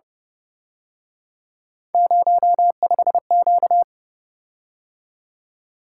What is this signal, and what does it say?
Morse code, heard as a keyed tone, about 29 wpm, 700 Hz. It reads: 05Q
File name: Speed-Racing - Three Letter-Number Combinations - 25wpm